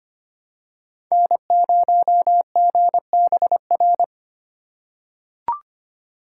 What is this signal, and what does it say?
Morse code, heard as a keyed tone, about 25 wpm, 700 Hz. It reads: N0GBR
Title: USA Call Signs - 2x Character Spacing - 25wpm